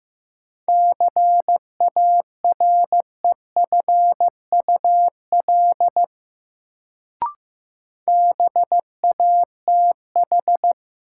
Morse code, heard as a keyed tone, about 15 wpm, 700 Hz. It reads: CAREFUL BATH
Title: Sentences from Top 1600 Words - New Word Drill - 15wpm